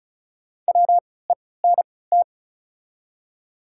Morse code, heard as a keyed tone, about 36 wpm, spaced as for 20 wpm, 700 Hz. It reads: WENT
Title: 3x Character Spacing - Top 200 Words - 35wpm